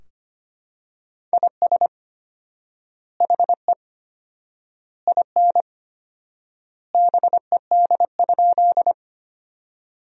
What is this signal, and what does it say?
Morse code, heard as a keyed tone, about 25 wpm, 700 Hz. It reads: IS HE IN BED?
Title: 4x Word-Spacing - Sentences from Top 500 Words - 25wpm